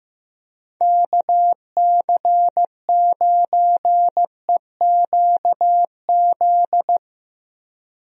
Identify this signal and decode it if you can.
Morse code, heard as a keyed tone, about 15 wpm, 700 Hz. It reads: KC9EQZ
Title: USA Call Signs - 6 Characters - 15wpm